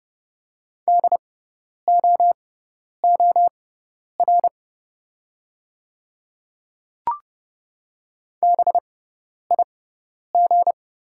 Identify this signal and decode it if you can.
Morse code, heard as a keyed tone, about 31 wpm, spaced as for 10 wpm, 700 Hz. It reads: DOOR BIG
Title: 6x Character Spacing - Top 300 Words - 30wpm